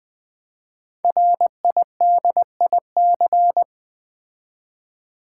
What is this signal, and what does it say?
Morse code, heard as a keyed tone, about 20 wpm, 700 Hz. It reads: RIDIC
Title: Build A Word - 5 Letters At A Time - 10-Letter Words - 20wpm